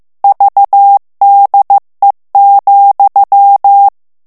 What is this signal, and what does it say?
Morse code, heard as a keyed tone, about 15 wpm, 800 Hz. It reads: VDE,
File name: ALLEMAND texte clair 1 vitesse 14 mots minute